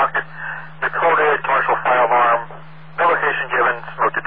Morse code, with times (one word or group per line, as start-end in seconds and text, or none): none